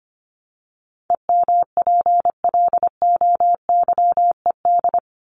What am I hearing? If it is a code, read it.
Morse code, heard as a keyed tone, about 25 wpm, 700 Hz. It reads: EMPLOYED
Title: ED Suffix - Top 100 Words with Base Words 25wpm